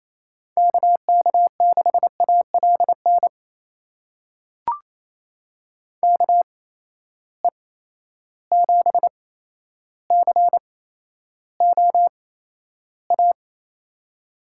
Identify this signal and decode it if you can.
Morse code, heard as a keyed tone, about 28 wpm, spaced as for 12 wpm, 700 Hz. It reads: KK6ALN KE7COA
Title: USA Call Signs - 6 Characters - 8x Character Spacing - 28wpm